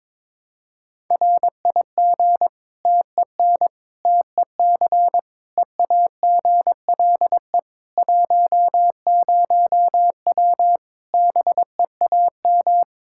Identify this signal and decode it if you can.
Morse code, heard as a keyed tone, about 22 wpm, 700 Hz. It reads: RIG TEN TEC EAGLE 10W BEAM
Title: Practice QSO - Exchange 2 - Info - 22wpm